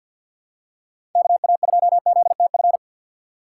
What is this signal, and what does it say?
Morse code, heard as a keyed tone, about 47 wpm, 700 Hz. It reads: KA2ZTF
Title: USA Call Signs - 6x Character Spacing - 50wpm